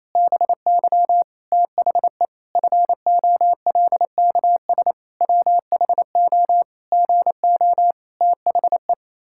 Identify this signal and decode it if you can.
Morse code, heard as a keyed tone, about 28 wpm, 700 Hz. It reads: BY THE FOLKS WHO GO THE
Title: Pangram Challenge v2 - 28wpm